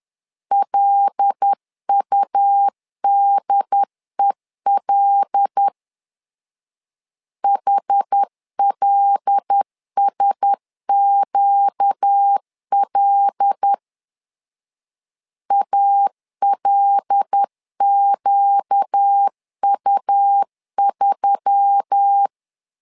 Morse code, heard as a keyed tone, about 10 wpm, 800 Hz. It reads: LUDEL HLSQL ALQU3
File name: LEZIONI / 5 LBVC / 5 L 10wpm 2259